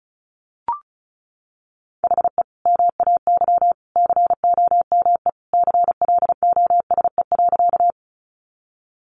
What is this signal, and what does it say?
Morse code, heard as a keyed tone, about 34 wpm, 700 Hz. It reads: HE MAY COME CLOSE.
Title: Sentences from Top 300 Words 35wpm